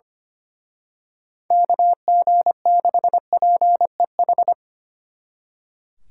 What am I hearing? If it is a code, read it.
Morse code, heard as a keyed tone, about 27 wpm, 700 Hz. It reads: KG6PEH
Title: Speed Racing - USA Call Signs 25wpm